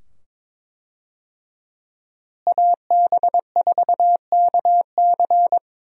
Morse code, heard as a keyed tone, about 22 wpm, 700 Hz. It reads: AB4KC
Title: USA Call Signs - 5 Characters - Repeated 2x - 22wpm